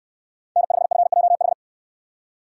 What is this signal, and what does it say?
Morse code, heard as a keyed tone, about 69 wpm, 700 Hz. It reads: N5FJH